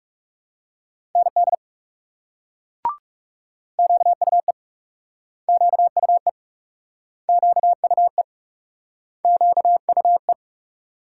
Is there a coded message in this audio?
Yes: Morse code from 1.2 to 10.3 s, heard as a keyed tone, about 38 wpm, 700 Hz. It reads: ND QUE QUE QUE QUE